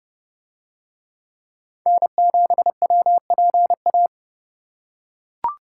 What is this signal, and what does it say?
Morse code, heard as a keyed tone, about 29 wpm, 700 Hz. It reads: N7WPA